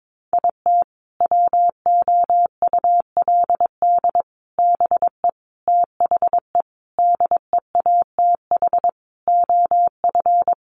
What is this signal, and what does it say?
Morse code, heard as a keyed tone, about 22 wpm, 700 Hz. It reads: IT WOULD BE THE DEATH OF
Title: Sentences from Top 1400 Words 22wpm